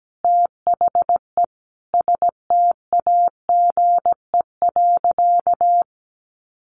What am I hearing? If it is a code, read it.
Morse code, heard as a keyed tone, about 17 wpm, 700 Hz. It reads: THE STAGE.
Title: Sentences from Top 1400 Words 17wpm